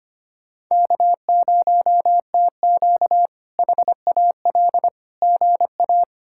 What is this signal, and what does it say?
Morse code, heard as a keyed tone, about 25 wpm, 700 Hz. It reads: K0TQ HAL GA
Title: K1USN SST QSO Practice - 25wpm